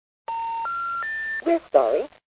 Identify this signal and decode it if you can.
Special information tone, with speech.